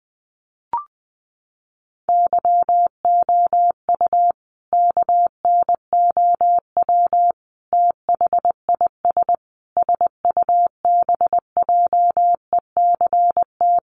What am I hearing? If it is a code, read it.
Morse code, heard as a keyed tone, about 20 wpm, 700 Hz. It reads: YOU KNOW THIS SUBJECT
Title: Sentences from Top 800 Words 20wpm